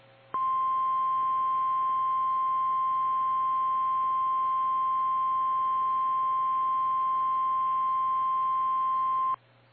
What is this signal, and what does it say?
Emergency Alert System (SAME) tones (the 1050 Hz attention tone). It is heard from 0.3 to 9.4 s.
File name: eas / KKPRKXPN / KKPRKXPN